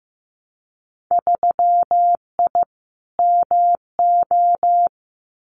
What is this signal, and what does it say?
Morse code, heard as a keyed tone, about 15 wpm, 700 Hz. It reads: 3I MO
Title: Winter Field Day Exchanges 15wpm